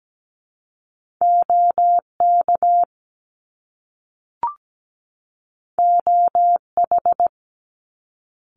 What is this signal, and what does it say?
Morse code, heard as a keyed tone, about 17 wpm, 700 Hz. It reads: OK OH
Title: Top Two Letter Combinations in Words 17wpm